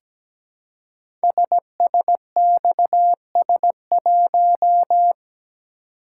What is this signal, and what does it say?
Morse code, heard as a keyed tone, about 17 wpm, 700 Hz. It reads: SSXS1
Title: Code Groups - 5 Letter:Numbers - 17wpm